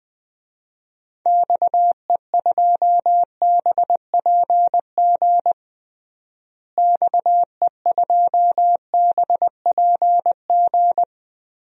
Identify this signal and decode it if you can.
Morse code, heard as a keyed tone, about 21 wpm, 700 Hz. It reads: XE2BPG XE2BPG
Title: DXCC-100 Call Signs - 6 Characters - Repeated 3x - 20wpm